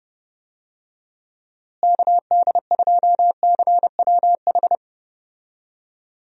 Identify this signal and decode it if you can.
Morse code, heard as a keyed tone, about 30 wpm, 700 Hz. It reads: KD2CWH
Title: USA Call Signs - 6 Characters - Repeated 3x - 30wpm